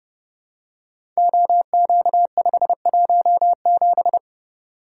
Morse code, heard as a keyed tone, about 28 wpm, 700 Hz. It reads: OQ517
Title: Code Groups - 5 Letter:Numbers - 30wpm